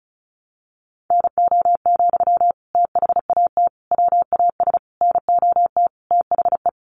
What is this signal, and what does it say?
Morse code, heard as a keyed tone, about 34 wpm, 700 Hz. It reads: NO, THAT WAS NOT THE
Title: Sentences from Top 600 Words 35wpm